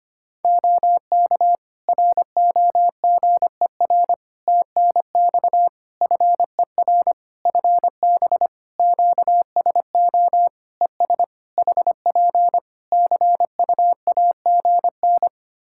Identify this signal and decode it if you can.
Morse code, heard as a keyed tone, about 25 wpm, 700 Hz. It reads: OK ROGER TNX FER FB QSO ES HP CUAGN
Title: Practice QSO - Exchange 4 - Ending - 25wpm